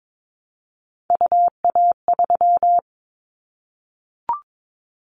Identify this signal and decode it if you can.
Morse code, heard as a keyed tone, about 22 wpm, 700 Hz. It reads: UA3
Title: Callsign Prefixes for Top 25 DXCC - Repeated 3x - 22wpm